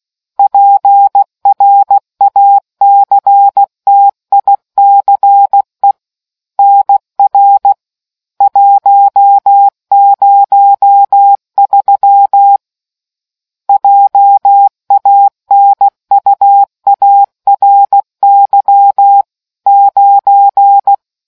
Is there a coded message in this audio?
Yes: Morse code from 0.4 to 20.9 s, heard as a keyed tone, about 16 wpm, 800 Hz. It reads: PRACTICE NR 103 JANUARY 9